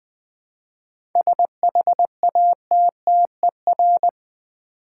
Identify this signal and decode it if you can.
Morse code, heard as a keyed tone, about 20 wpm, 700 Hz. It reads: SHATTER
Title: 7 Letter Words - First 3 Letters Given - 20wpm